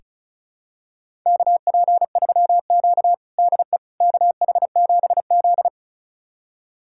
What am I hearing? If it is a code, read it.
Morse code, heard as a keyed tone, about 36 wpm, 700 Hz. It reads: KP3Q DE KH7Z
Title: USA Callsign DE Callsign - 4 Characters - 35wpm